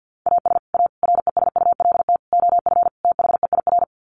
Morse code, heard as a keyed tone, about 49 wpm, 700 Hz. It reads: AS A RESULT OF THEIR
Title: Sets of 5 Words Encore2 50wpm